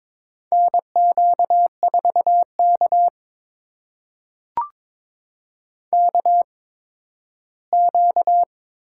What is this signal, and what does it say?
Morse code, heard as a keyed tone, about 22 wpm, spaced as for 13 wpm, 700 Hz. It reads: NQ4K KQ
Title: USA Call Signs - 4 Characters - 8x Character Spacing - 22wpm